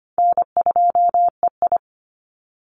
Morse code, heard as a keyed tone, about 25 wpm, 700 Hz. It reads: N2EI
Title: Easy Call Signs Repeated 2x - 25wpm